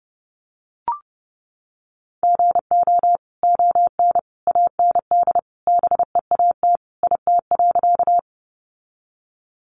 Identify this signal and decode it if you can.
Morse code, heard as a keyed tone, about 30 wpm, 700 Hz. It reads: GO ON AND BEAT IT.
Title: Sentences from Top 1300 Words 30wpm